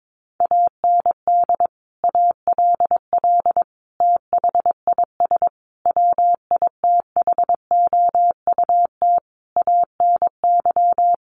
Morse code, heard as a keyed tone, about 22 wpm, 700 Hz. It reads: AND ALL THIS WITHOUT ANY
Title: Sentences from Top 1200 Words 22wpm